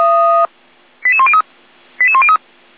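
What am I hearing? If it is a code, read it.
Five-tone selective call (ZVEI), heard as quick runs of five tones, tones of about 70 ms. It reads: 88103 88103
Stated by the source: ZVEI five-tone selective call received on an SDR.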